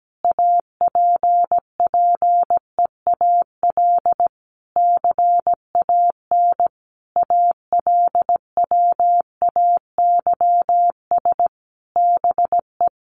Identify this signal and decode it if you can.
Morse code, heard as a keyed tone, about 17 wpm, 700 Hz. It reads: APPEAL CAN ALWAYS BE
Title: Sentences from Top 1300 Words 17wpm